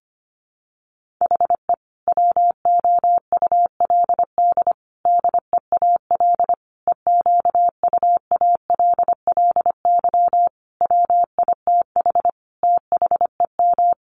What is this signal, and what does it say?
Morse code, heard as a keyed tone, about 25 wpm, 700 Hz. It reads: HE WOULD DEAL EQUALLY WITH THEM
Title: Sentences from Top 1400 Words 25wpm